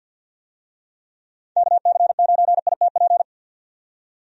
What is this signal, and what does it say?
Morse code, heard as a keyed tone, about 47 wpm, 700 Hz. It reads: KC9ITP
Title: USA Call Signs - 6 Characters - 50wpm